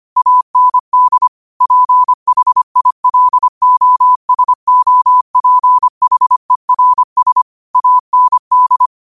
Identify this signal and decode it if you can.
Morse code, heard as a keyed tone, about 25 wpm, 1000 Hz. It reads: AND PHILOSOPHERS AND
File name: qotdfiles / 25wpm